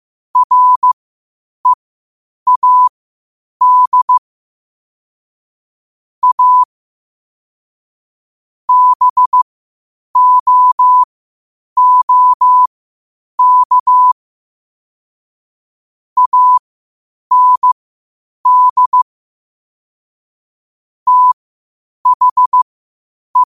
Morse code, heard as a keyed tone, about 15 wpm, spaced as for 8 wpm, 1000 Hz. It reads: READ A BOOK AND THE
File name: qotdfiles / 8wpm